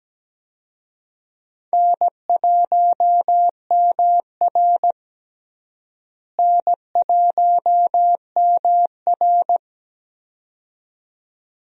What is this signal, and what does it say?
Morse code, heard as a keyed tone, about 17 wpm, 700 Hz. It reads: N1MR N1MR
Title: USA Call Signs - 4 Characters - Repeated 2x - 17wpm